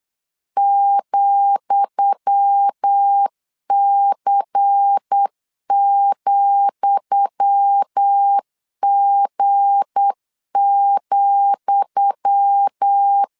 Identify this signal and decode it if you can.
Morse code, heard as a keyed tone, about 8 wpm, 800 Hz. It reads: ,C,G,
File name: LEZIONI / 10 SIGNS / 10D COMA 8wpm 3749